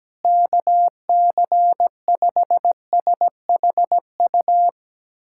Morse code, heard as a keyed tone, about 17 wpm, 700 Hz. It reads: KC5SHU
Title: USA Callsign DE Callsign - 6 Characters - 17wpm